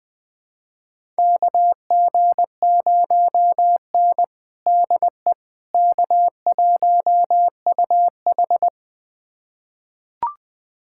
Morse code, heard as a keyed tone, about 20 wpm, 700 Hz. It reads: KG0N DE K1UH
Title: USA Callsign DE Callsign - 4 Characters - 20wpm